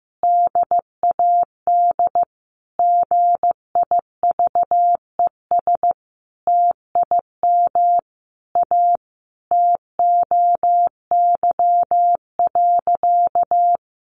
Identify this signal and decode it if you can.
Morse code, heard as a keyed tone, about 15 wpm, 700 Hz. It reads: DAD GIVES TIM A TOY.
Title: Easy Phrases 15wpm